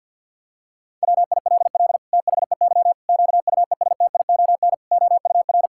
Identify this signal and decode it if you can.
Morse code, heard as a keyed tone, about 50 wpm, 700 Hz. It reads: WILL THEY QUESTION OUR
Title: Sentences from Top 1000 Words - Review Part 5 - 50wpm